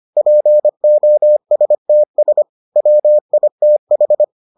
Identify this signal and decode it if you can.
Morse code, heard as a keyed tone, about 25 wpm, 600 Hz. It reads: POSTS WITH